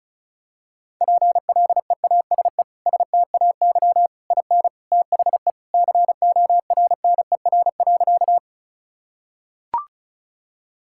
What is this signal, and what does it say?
Morse code, heard as a keyed tone, about 36 wpm, 700 Hz. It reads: PLEASE STAY IN THE CORNER.
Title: Sentences from Top 1600 Words 35wpm